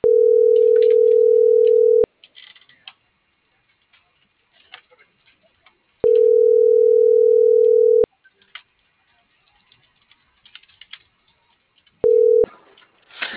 Ringback tone (North America), 440 + 480 Hz.